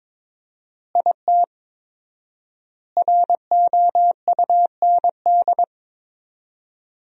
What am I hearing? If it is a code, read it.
Morse code, heard as a keyed tone, about 22 wpm, 700 Hz. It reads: IT ROUND
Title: 4x Word-Spacing - Sentences from Top 500 Words - 22wpm